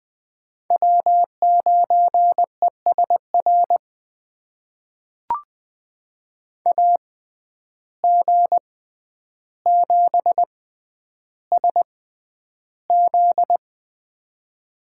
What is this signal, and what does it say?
Morse code, heard as a keyed tone, about 20 wpm, spaced as for 10 wpm, 700 Hz. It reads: W9ESR AG7SZ
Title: USA Call Signs - 6x Character Spacing - 20wpm